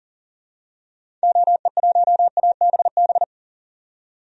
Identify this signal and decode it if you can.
Morse code, heard as a keyed tone, about 39 wpm, 700 Hz. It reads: OE1ABB
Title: DXCC-100 Call Signs - 6 Characters - 40wpm